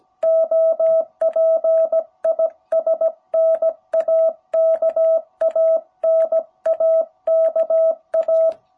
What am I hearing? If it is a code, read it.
Morse code, heard as a keyed tone, about 17 wpm, 660 Hz. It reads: OP IS NAKA NAKA
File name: jh8cbh / 2021cw